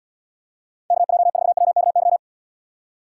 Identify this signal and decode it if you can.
Morse code, heard as a keyed tone, about 78 wpm, 700 Hz. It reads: KJ4WKY